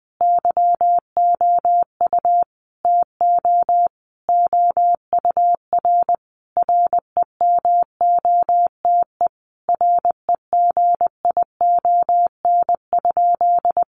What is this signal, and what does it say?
Morse code, heard as a keyed tone, about 20 wpm, 700 Hz. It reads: YOU TO OUR REMOTE REGION?
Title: Sentences from Top 1500 Words 20wpm